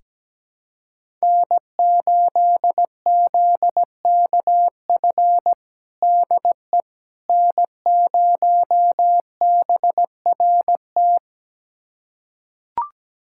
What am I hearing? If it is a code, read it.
Morse code, heard as a keyed tone, about 17 wpm, 700 Hz. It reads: N8ZKF DE N0BRT
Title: USA Callsign DE Callsign - 5 Characters - 17wpm